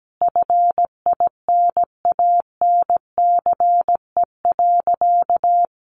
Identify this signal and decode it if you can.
Morse code, heard as a keyed tone, about 17 wpm, 700 Hz. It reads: FINANCE.
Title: Sentences from Top 1500 Words 17wpm